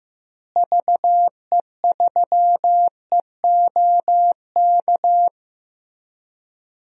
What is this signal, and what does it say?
Morse code, heard as a keyed tone, about 15 wpm, 700 Hz. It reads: VE3EOK
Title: DXCC-100 Call Signs - 6 Characters - Repeated 3x - 15wpm